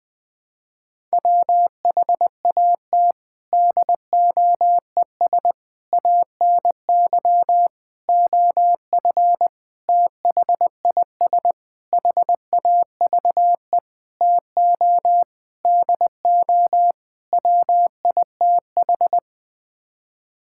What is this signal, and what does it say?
Morse code, heard as a keyed tone, about 20 wpm, 700 Hz. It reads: WHAT DOES ANY OF THIS HAVE TO DO WITH
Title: Sets of 9 Words - 20wpm